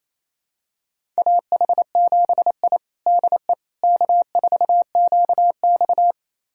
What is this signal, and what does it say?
Morse code, heard as a keyed tone, about 28 wpm, 700 Hz. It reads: AH7I DE K4QX
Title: USA Callsign DE Callsign - 4 Characters - 28wpm